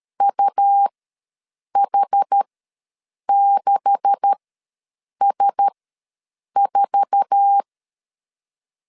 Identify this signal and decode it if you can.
Morse code, heard as a keyed tone, about 13 wpm, spaced as for 7 wpm, 800 Hz. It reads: UH6S4
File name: LEZIONI / 9 286 ws 12wpm 3351